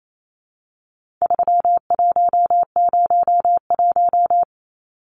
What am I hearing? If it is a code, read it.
Morse code, heard as a keyed tone, about 28 wpm, 700 Hz. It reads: 3101
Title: Numbers - 4-Digit 28wpm